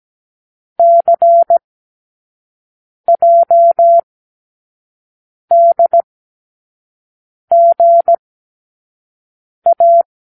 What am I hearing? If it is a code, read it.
Morse code, heard as a keyed tone, about 17 wpm, spaced as for 5 wpm, 680 Hz. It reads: CJDGA
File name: k7qo / Lesson15(A-J)